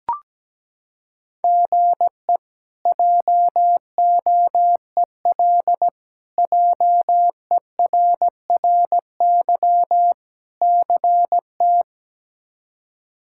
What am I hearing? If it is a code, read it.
Morse code, heard as a keyed tone, about 17 wpm, 700 Hz. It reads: GE JOEL JERRY CT
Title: K1USN SST QSO Practice - 17wpm